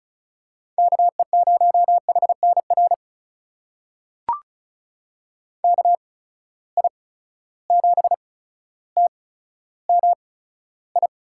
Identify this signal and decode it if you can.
Morse code, heard as a keyed tone, about 34 wpm, spaced as for 15 wpm, 700 Hz. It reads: KE0HNR KI7TMI
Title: USA Call Signs - 6 Characters - 8x Character Spacing - 35wpm